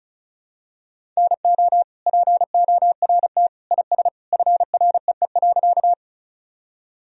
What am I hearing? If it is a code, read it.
Morse code, heard as a keyed tone, about 36 wpm, 700 Hz. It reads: NO PORT IS FREE.
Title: Sentences from Top 1000 Words - Review Part 1 - 35wpm